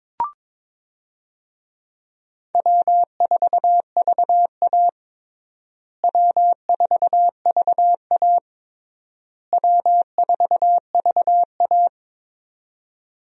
Morse code, heard as a keyed tone, about 22 wpm, 700 Hz. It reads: W4VA W4VA W4VA
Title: USA Call Signs - 4 Characters - Repeated 3x - 22wpm